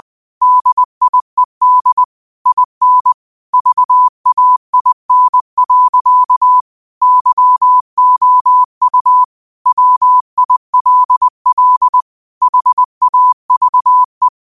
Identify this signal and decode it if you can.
Morse code, heard as a keyed tone, about 20 wpm, 1000 Hz. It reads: DIED IN VAIN. YOU WILL HAVE